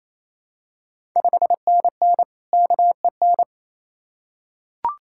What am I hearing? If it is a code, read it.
Morse code, heard as a keyed tone, about 28 wpm, 690 Hz. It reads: 5NN KEN
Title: Ontario QSO Party Exchange 28wpm